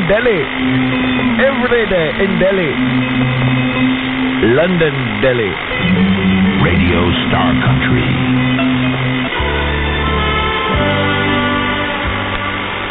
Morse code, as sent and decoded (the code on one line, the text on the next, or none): none